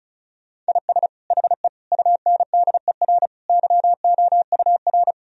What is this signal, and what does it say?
Morse code, heard as a keyed tone, about 34 wpm, 700 Hz. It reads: IS HE UNDER YOUR
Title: Sentences from Top 1000 Words - Review Part 1 - 35wpm